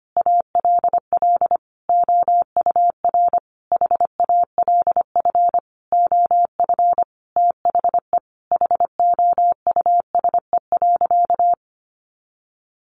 Morse code, heard as a keyed tone, about 25 wpm, 700 Hz. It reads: ALL OUR HALF OF THE HOUSE.
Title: Sentences from Top 400 Words 25wpm